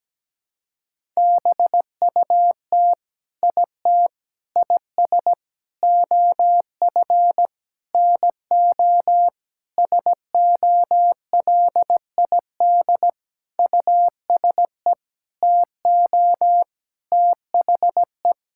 Morse code, heard as a keyed tone, about 17 wpm, 700 Hz. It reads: BUT IT IS OF NO SOLID USE TO THE
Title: Sentences from Top 1600 Words 17wpm